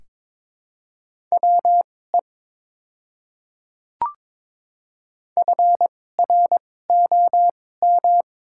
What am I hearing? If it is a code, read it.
Morse code, heard as a keyed tone, about 22 wpm, spaced as for 16 wpm, 700 Hz. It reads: WE FROM
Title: 2x Character Spacing - Top 100 Words - 22wpm